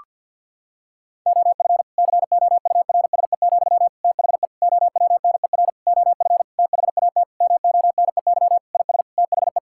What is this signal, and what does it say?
Morse code, heard as a keyed tone, about 50 wpm, 700 Hz. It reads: OF COURSE, THE OWNER OF THAT MONEY IS THE
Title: Sentences from Top 1600 Words 50wpm